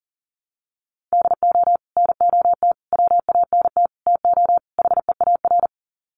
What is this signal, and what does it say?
Morse code, heard as a keyed tone, about 40 wpm, 700 Hz. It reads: DO NOT WANT TO HEAR
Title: Sets of 5 Words 40wpm